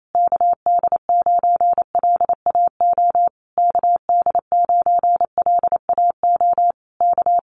Morse code, heard as a keyed tone, about 28 wpm, 700 Hz. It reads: KD9LAO KD9LAO K
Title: Common QSO Phrases Encore 28wpm